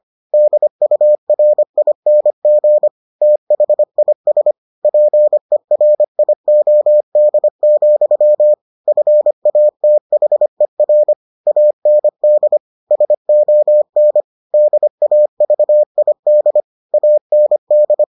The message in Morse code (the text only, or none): DURING THIS PERIOD, FATHER AND SON DAVID AND